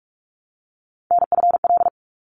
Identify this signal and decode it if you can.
Morse code, heard as a keyed tone, about 44 wpm, 700 Hz. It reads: NFL
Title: ARRL Sections 45wpm